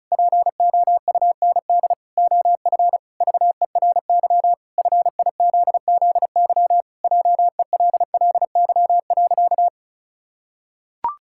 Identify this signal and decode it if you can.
Morse code, heard as a keyed tone, about 36 wpm, 700 Hz. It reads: POUND OF VERY FIZZY JELLY.
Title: Pangram Challenge v2 - 35wpm